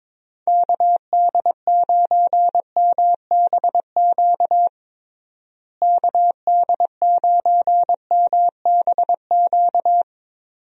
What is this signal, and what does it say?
Morse code, heard as a keyed tone, about 22 wpm, 700 Hz. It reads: KD9MBQ KD9MBQ
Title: USA Call Signs - 6 Characters - Repeated 2x - 22wpm